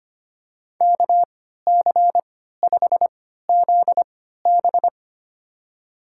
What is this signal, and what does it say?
Morse code, heard as a keyed tone, about 25 wpm, spaced as for 14 wpm, 700 Hz. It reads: KC5ZB
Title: USA Call Signs - 3x Character Spacing - 25wpm